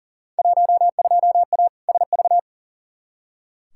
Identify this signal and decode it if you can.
Morse code, heard as a keyed tone, about 38 wpm, 700 Hz. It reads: 12A SV
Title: Field Day Exchanges Encore - 40wpm